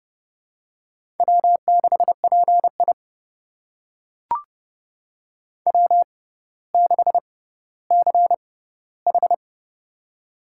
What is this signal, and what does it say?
Morse code, heard as a keyed tone, about 30 wpm, spaced as for 15 wpm, 700 Hz. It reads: W6PI W6CH
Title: USA Call Signs - 4 Characters - 6x Character Spacing - 30wpm